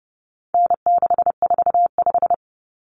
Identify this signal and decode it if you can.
Morse code, heard as a keyed tone, about 30 wpm, 700 Hz. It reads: N645
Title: Common Cut Numbers - 4-Digit 30wpm